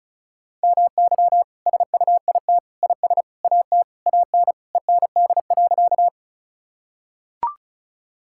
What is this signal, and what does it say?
Morse code, heard as a keyed tone, about 36 wpm, 700 Hz. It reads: MY SUIT IS AT AN END.
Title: Sentences from Top 1600 Words 35wpm